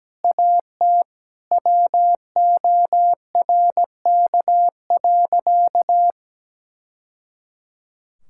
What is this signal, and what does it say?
Morse code, heard as a keyed tone, about 17 wpm, 700 Hz. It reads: AT WORK.
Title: Sentences from Top 1000 Words - Review Part 6 - 17wpm